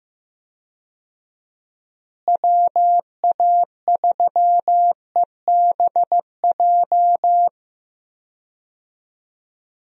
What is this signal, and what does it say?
Morse code, heard as a keyed tone, about 15 wpm, 700 Hz. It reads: WA3EBJ
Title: USA Call Signs - 6 Characters - Repeated 3x - 15wpm